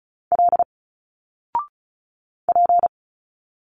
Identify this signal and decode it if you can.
Morse code, heard as a keyed tone, about 34 wpm, 700 Hz. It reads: L P
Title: Single Letters 35wpm